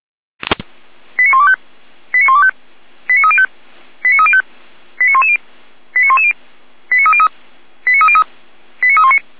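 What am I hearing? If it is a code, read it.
Five-tone selective call (ZVEI), heard as quick runs of five tones, tones of about 70 ms. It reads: 89125 89125 89305 89305 89110 89110 89303 89303 89319
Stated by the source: Five-tone ZVEI selective call received on an SDR.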